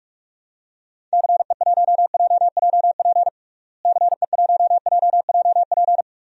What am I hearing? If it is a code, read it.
Morse code, heard as a keyed tone, about 46 wpm, 700 Hz. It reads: CE1JJP CE1JJP
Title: DXCC-100 Call Signs - 6 Characters - Repeated 2x - 45wpm